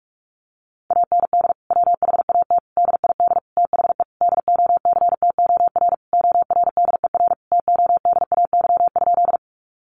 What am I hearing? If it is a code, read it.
Morse code, heard as a keyed tone, about 44 wpm, 700 Hz. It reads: AND WHAT DID THE DOCTOR ORDER TODAY?
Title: Sentences from Top 1500 Words 45wpm